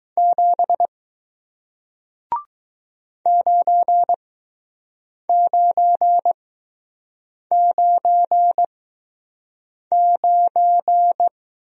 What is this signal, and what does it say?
Morse code, heard as a keyed tone, about 20 wpm, spaced as for 6 wpm, 700 Hz. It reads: 7 9999